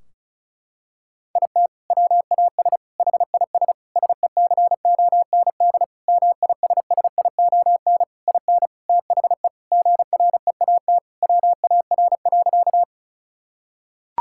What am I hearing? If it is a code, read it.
Morse code, heard as a keyed tone, about 36 wpm, 700 Hz. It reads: IT WAS HIS SECOND MISSION IN THE GREAT WAR.
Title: Sentences from Top 1600 Words 35wpm